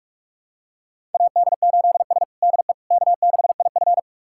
Morse code, heard as a keyed tone, about 41 wpm, 700 Hz. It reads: AD8S DE K6IF
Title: USA Callsign DE Callsign - 4 Characters - 45wpm